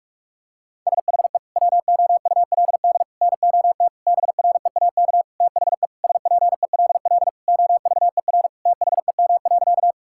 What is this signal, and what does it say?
Morse code, heard as a keyed tone, about 46 wpm, 700 Hz. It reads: SHE WOULD NOT BREAK THE SPELL OVER THEM.
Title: Sentences from Top 1000 Words - Review Part 6 - 45wpm